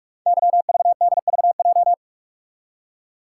Morse code, heard as a keyed tone, about 44 wpm, 690 Hz. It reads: YVDVJ
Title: Code Groups - 5 Letters - 45wpm